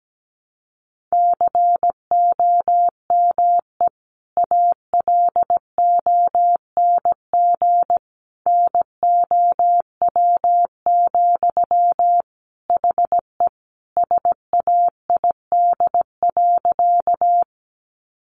Morse code, heard as a keyed tone, about 17 wpm, 700 Hz. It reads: COME ALONG NOW, HE SAID.
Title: Sentences from Top 700 Words 17wpm